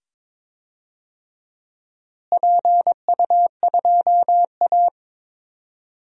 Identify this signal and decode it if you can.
Morse code, heard as a keyed tone, about 22 wpm, 700 Hz. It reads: PU2A
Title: DXCC-100 Call Signs - 4 Characters - Repeated 2x - 22wpm